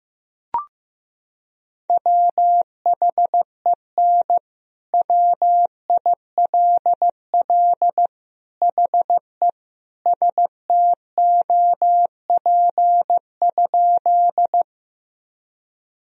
Morse code, heard as a keyed tone, about 16 wpm, 700 Hz. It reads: WHEN WILL HE STOP?
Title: Sentences from Top 1000 Words - Review Part 1 - 15wpm